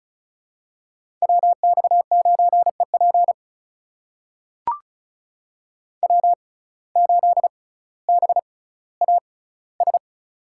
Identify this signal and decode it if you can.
Morse code, heard as a keyed tone, about 36 wpm, spaced as for 18 wpm, 700 Hz. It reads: WX9EP W8BAS
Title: USA Call Signs - 5 Characters - 6x Character Spacing - 35wpm